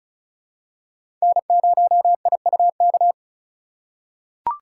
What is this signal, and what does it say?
Morse code, heard as a keyed tone, about 34 wpm, 700 Hz. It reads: N0IUK
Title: USA Call Signs - 5 Characters - 6x Character Spacing - 35wpm